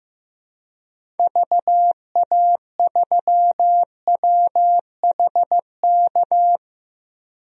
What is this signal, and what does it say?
Morse code, heard as a keyed tone, about 15 wpm, 700 Hz. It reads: VA3WHK
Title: DXCC-100 Call Signs - 6 Characters - Repeated 2x - 15wpm